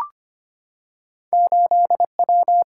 Morse code, heard as a keyed tone, about 25 wpm, 700 Hz. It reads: 8W
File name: Two Letter-Number Combinations - 25wpm